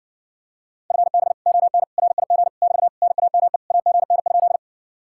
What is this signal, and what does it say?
Morse code, heard as a keyed tone, about 60 wpm, 700 Hz. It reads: FB ON RIG <BT> NAME AGN?